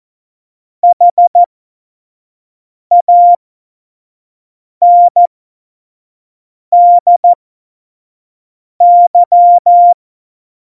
Morse code, heard as a keyed tone, about 14 wpm, spaced as for 5 wpm, 700 Hz. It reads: HANDY
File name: Morsekurs / Witze / Witz 28 April 2020, 25 F70